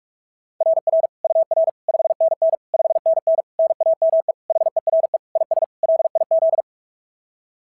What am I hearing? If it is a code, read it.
Morse code, heard as a keyed tone, about 46 wpm, 690 Hz. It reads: RR UR 5NN 5NN NAME HERE IS LIZ
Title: SOTA QSO Practice 45wpm